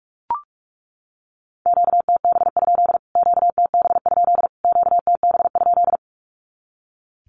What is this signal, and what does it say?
Morse code, heard as a keyed tone, about 44 wpm, 700 Hz. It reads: QTB? QTB? QTB?
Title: Q-Codes Repeated 3x 45wpm